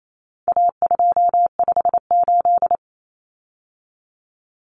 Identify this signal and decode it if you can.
Morse code, heard as a keyed tone, about 28 wpm, 700 Hz. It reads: A258
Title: Common Cut Numbers - 4-Digit 28wpm